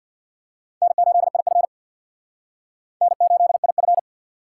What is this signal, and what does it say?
Morse code, heard as a keyed tone, about 55 wpm, 700 Hz. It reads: N8IF N8IF
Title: Speed Racing - USA 1x2 Call Signs 50wpm